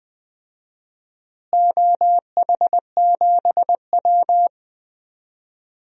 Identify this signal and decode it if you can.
Morse code, heard as a keyed tone, about 20 wpm, 700 Hz. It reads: OH7W
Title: DXCC-100 Call Signs - 4 Characters - Repeated 2x - 20wpm